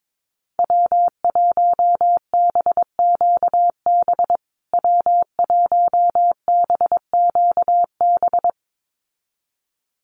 Morse code, heard as a keyed tone, about 22 wpm, 700 Hz. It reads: W1BQB W1BQB
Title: USA Call Signs Repeated 2x 22wpm